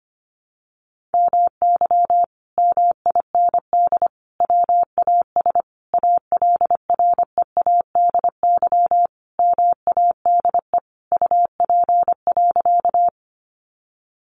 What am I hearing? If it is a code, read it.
Morse code, heard as a keyed tone, about 25 wpm, 700 Hz. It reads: MY MIND WAS ALREADY MADE UP.
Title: Sentences from Top 700 Words 25wpm